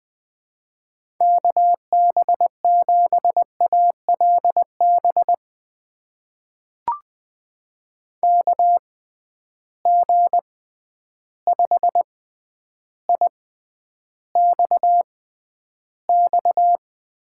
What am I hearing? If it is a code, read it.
Morse code, heard as a keyed tone, about 20 wpm, spaced as for 10 wpm, 700 Hz. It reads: KB7ALB KG5IXX
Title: USA Call Signs - 6x Character Spacing - 20wpm